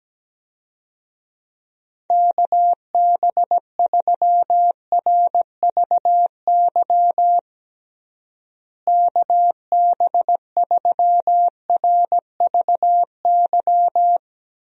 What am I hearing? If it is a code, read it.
Morse code, heard as a keyed tone, about 17 wpm, 700 Hz. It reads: KB3RVY KB3RVY
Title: USA Call Signs - 6 Characters - Repeated 3x - 17wpm